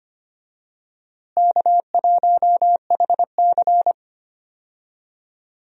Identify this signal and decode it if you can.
Morse code, heard as a keyed tone, about 25 wpm, 700 Hz. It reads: K1HC
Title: USA Call Signs - 4 Characters - Repeated 2x - 25wpm